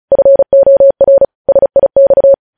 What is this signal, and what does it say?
Morse code, heard as a keyed tone, about 36 wpm, 550 Hz. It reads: FOR SIX
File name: podcast / News headlines-1x-35x35